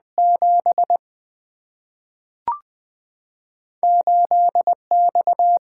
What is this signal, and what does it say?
Morse code, heard as a keyed tone, about 20 wpm, 700 Hz. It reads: 7 8 X
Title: Single-Letter Single-Number 20wpm